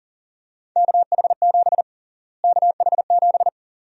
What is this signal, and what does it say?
Morse code, heard as a keyed tone, about 37 wpm, 700 Hz. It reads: KH7 KH7
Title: USA Call Sign Prefixes - Repeated 2x - 40wpm